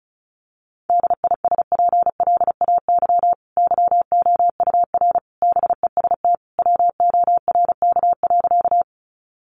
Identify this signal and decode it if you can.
Morse code, heard as a keyed tone, about 34 wpm, 700 Hz. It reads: DISPLAY YOUR BEST WORK.
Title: Sentences from Top 1200 Words 35wpm